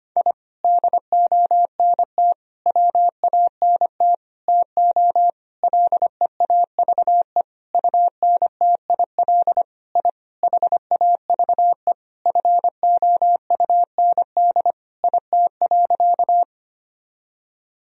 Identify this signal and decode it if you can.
Morse code, heard as a keyed tone, about 25 wpm, 700 Hz. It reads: I DONT WANT TO LEAVE UNTIL I HAVE FOUND IT.
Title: Sentences from Top 1000 Words - Review Part 1 - 25wpm